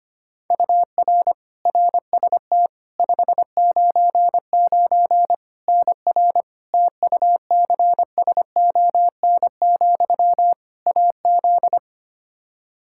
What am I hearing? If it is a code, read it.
Morse code, heard as a keyed tone, about 25 wpm, 700 Hz. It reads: UR RST 599 NR TUCSON, AZ
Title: Practice QSO - Exchange 1 - Beginning - 25wpm